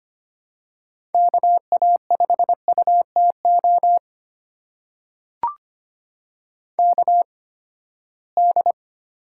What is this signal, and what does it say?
Morse code, heard as a keyed tone, about 25 wpm, spaced as for 17 wpm, 700 Hz. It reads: KA5UTO KD
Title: USA Call Signs - 6 Characters - 8x Character Spacing - 25wpm